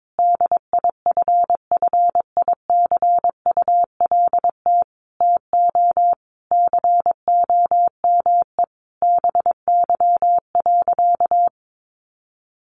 Morse code, heard as a keyed tone, about 22 wpm, 700 Hz. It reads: DIFFICULT TO COME BY.
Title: Sentences from Top 1500 Words 22wpm